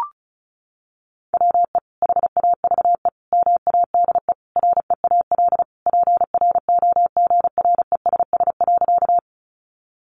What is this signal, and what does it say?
Morse code, heard as a keyed tone, about 34 wpm, 700 Hz. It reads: WE HAVE MADE REAL PROGRESS.
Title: Sentences from Top 1400 Words 35wpm